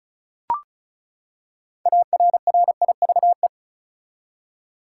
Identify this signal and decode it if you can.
Morse code, heard as a keyed tone, about 36 wpm, 700 Hz. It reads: ARRIVE
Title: Six Letter Words - 35wpm